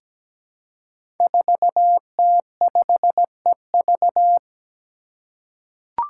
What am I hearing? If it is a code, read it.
Morse code, heard as a keyed tone, about 17 wpm, 700 Hz. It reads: 4T5EV
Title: Five Challenging Letter-Number Combinations 17wpm